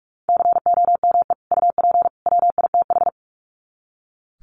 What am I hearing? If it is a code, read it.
Morse code, heard as a keyed tone, about 44 wpm, 700 Hz. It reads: COME UP WITH
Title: Sets of 3 Words 45wpm